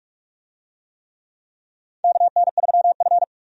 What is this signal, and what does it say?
Morse code, heard as a keyed tone, about 43 wpm, 700 Hz. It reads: KN3F
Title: USA Call Signs - 4 Characters - 45wpm